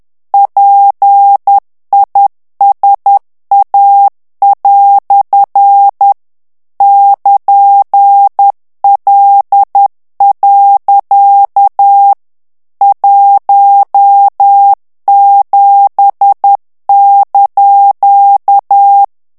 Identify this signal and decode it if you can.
Morse code, heard as a keyed tone, about 10 wpm, 800 Hz. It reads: PISA" <KN>L. 17)
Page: Français - texte en clair 1 - vitesse : 10 mots minute